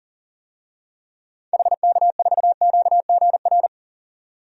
Morse code, heard as a keyed tone, about 40 wpm, 700 Hz. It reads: HK4QGR
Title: DXCC-100 Call Signs - 6 Characters - 40wpm